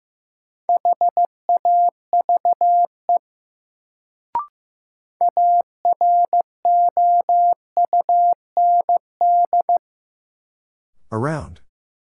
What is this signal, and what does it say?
Morse code, heard as a keyed tone, about 15 wpm, 700 Hz. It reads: HAVE AROUND
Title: Sets of 2 Words - New Word Drill - 15wpm